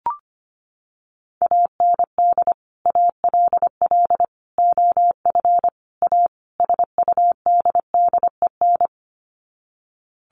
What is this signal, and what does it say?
Morse code, heard as a keyed tone, about 25 wpm, 700 Hz. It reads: AND ALL OF A SUDDEN
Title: Sets of 5 Words 25wpm